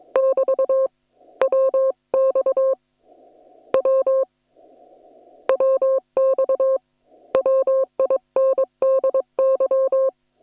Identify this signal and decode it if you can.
Morse code, heard as a keyed tone, about 22 wpm, 530 Hz. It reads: <BT> WX W WX WINDY
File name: attachments / hf 1 001